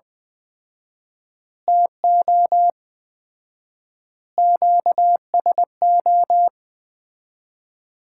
Morse code, heard as a keyed tone, about 20 wpm, 700 Hz. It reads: TO QSO
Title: Common QSO Phrases - 4x Word-Spacing - 20wpm